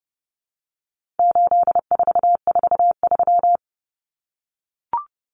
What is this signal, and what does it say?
Morse code, heard as a keyed tone, about 30 wpm, 700 Hz. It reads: 8443
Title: Common Cut Numbers - 4-Digit 30wpm